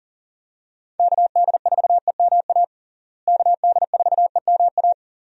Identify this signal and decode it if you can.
Morse code, heard as a keyed tone, about 40 wpm, 700 Hz. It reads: KD4EMA KD4EMA
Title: USA Call Signs - 6 Characters - Repeated 2x - 40wpm